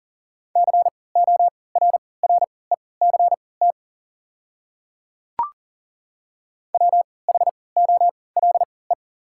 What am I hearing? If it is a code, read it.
Morse code, heard as a keyed tone, about 40 wpm, spaced as for 23 wpm, 690 Hz. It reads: CORRECT WHOLE
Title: 3x Character Spacing - Top 500 Words - 40wpm